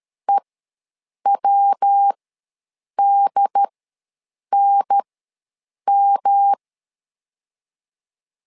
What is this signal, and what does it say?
Morse code, heard as a keyed tone, about 13 wpm, spaced as for 7 wpm, 800 Hz. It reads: EWDNM
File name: LEZIONI / 3 MWRO / 3 MW ws 12wpm 1816